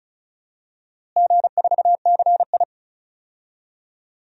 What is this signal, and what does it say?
Morse code, heard as a keyed tone, about 36 wpm, 700 Hz. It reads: G4CI
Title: DXCC-100 Call Signs - 4 Characters - Repeated 2x - 35wpm